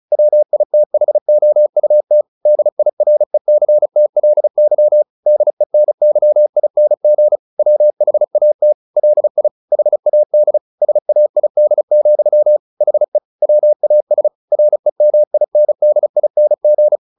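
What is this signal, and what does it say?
Morse code, heard as a keyed tone, about 34 wpm, 600 Hz. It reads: WITHOUT DIRECTLY DENYING WHAT LI HAD SAID, HE WAS REMINDING
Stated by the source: Morse practice at 35 wpm from CW med Gnister, sending Ken Follett's novel Never in English.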